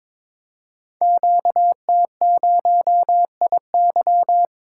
Morse code, heard as a keyed tone, about 22 wpm, 700 Hz. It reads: QT0IY
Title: Code Groups - 5 Letter:Numbers - 22wpm